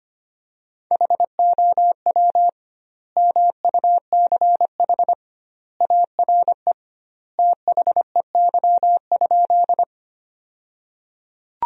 Morse code, heard as a keyed tone, about 25 wpm, 700 Hz. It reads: HOW MUCH ARE THEY?
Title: 2x Word-Spacing - Sentences from Top 200 Words - 25wpm